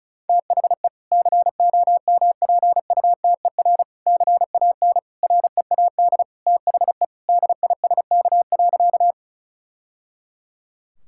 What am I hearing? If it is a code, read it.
Morse code, heard as a keyed tone, about 36 wpm, 700 Hz. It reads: THE COMPUTER CAN READ THE DISK.
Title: Sentences from Top 1000 Words - Review Part 6 - 35wpm